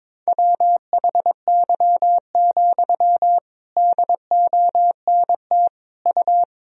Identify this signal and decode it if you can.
Morse code, heard as a keyed tone, about 22 wpm, 700 Hz. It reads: WHY, DONT U
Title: Sentences from Top 300 Words and Abbreviations 22wpm